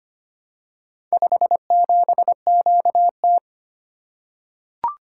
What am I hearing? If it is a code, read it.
Morse code, heard as a keyed tone, about 25 wpm, 700 Hz. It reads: 57QT
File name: practice / 25wpm / Four Letter-Number Combinations - 25wpm